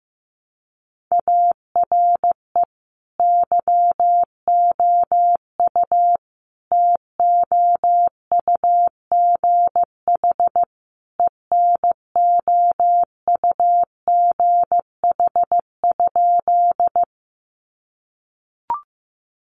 Morse code, heard as a keyed tone, about 15 wpm, 700 Hz. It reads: ARE YOU TOUGH ENOUGH?
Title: Sentences from Top 1200 Words 15wpm